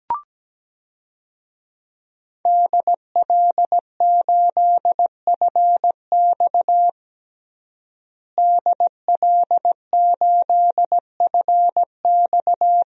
Morse code, heard as a keyed tone, about 17 wpm, 700 Hz. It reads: DL8FX DL8FX
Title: DXCC-100 Call Signs - 5 Characters - Repeated 3x - 17wpm